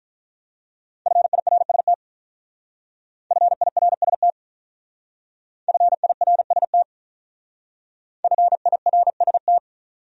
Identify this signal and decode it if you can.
Morse code, heard as a keyed tone, about 44 wpm, 700 Hz. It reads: FIRST FIRST FIRST FIRST
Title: Speed-Racing Top 500 Words 35wpm